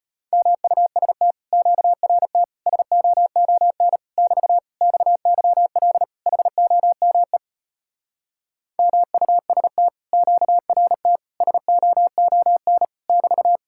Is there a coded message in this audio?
Yes: Morse code from 0.3 to 13.6 s, heard as a keyed tone, about 33 wpm, 700 Hz. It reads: MUST QRT SOON <BT> XYL HOME MUST QRT SOON <BT>